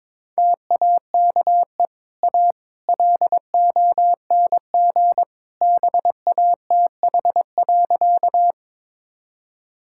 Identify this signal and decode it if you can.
Morse code, heard as a keyed tone, about 22 wpm, 690 Hz. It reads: TAKE A LONG BATH.
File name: practice / 22wpm / Easy Sentences with Hints - 22wpm